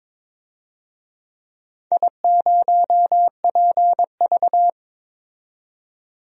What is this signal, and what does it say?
Morse code, heard as a keyed tone, about 22 wpm, 700 Hz. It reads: I0PV
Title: DXCC-100 Call Signs - 4 Characters - 22wpm